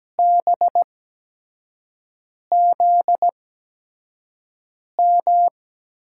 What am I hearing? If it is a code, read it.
Morse code, heard as a keyed tone, about 17 wpm, spaced as for 5 wpm, 700 Hz. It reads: BZM
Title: USA Call Signs - 6 Characters - 8x Character Spacing - 17wpm